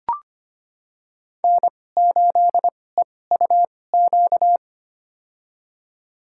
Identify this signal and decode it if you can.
Morse code, heard as a keyed tone, about 25 wpm, spaced as for 18 wpm, 700 Hz. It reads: N8EUQ
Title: USA Call Signs - 5 Characters - 2x Character Spacing - 25wpm